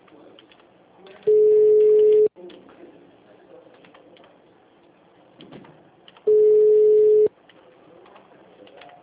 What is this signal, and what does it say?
Ringback tone (Europe, 425 Hz).
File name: out- -2014- - - .575